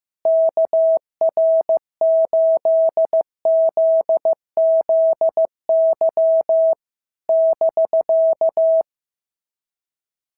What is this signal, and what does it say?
Morse code, heard as a keyed tone, about 15 wpm, 650 Hz. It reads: KR8ZZY <BK>
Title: SOTA QSO Practice 15wpm